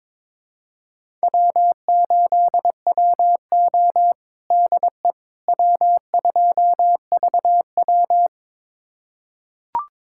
Morse code, heard as a keyed tone, about 22 wpm, 700 Hz. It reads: W8WO DE W2VW
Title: USA Callsign DE Callsign - 4 Characters - 22wpm